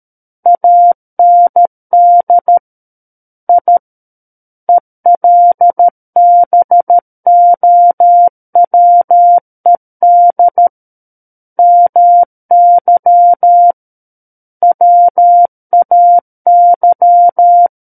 Morse code, heard as a keyed tone, about 13 wpm, 700 Hz. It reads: AND I ELBOWED MY WAY
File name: Book One / 04-Chapter 4 - 13 WPM